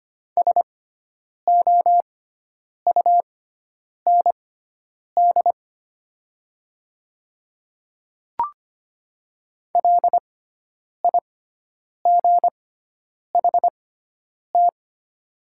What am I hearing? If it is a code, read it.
Morse code, heard as a keyed tone, about 25 wpm, spaced as for 9 wpm, 700 Hz. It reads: SOUND LIGHT
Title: 6x Character Spacing - Top 300 Words - 25wpm